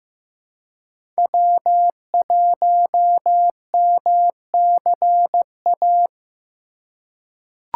Morse code, heard as a keyed tone, about 15 wpm, 700 Hz. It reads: W1MCA
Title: USA Call Signs - 5 Characters - 2x Character Spacing - 15wpm